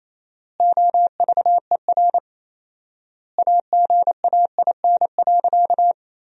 Morse code, heard as a keyed tone, about 28 wpm, 700 Hz. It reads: OVER AGAIN.
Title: 4x Word-Spacing - Sentences from Top 500 Words - 28wpm